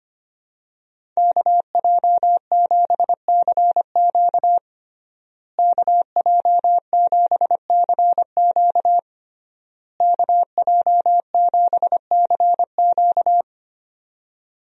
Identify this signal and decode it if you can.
Morse code, heard as a keyed tone, about 25 wpm, 700 Hz. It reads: KJ7CQ KJ7CQ KJ7CQ
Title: USA Call Signs - 5 Characters - Repeated 3x - 25wpm